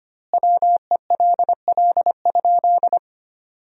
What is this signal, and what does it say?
Morse code, heard as a keyed tone, about 25 wpm, 700 Hz. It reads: WELL?
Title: 2x Word-Spacing - Sentences from Top 200 Words - 25wpm